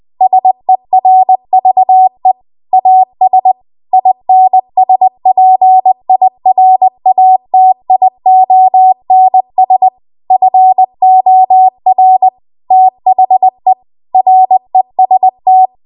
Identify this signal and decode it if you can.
Morse code, heard as a keyed tone, about 20 wpm, 750 Hz. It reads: SERVE AS INSPIRATIONS FOR THE REST
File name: Archive / 20 WPM / 20WPM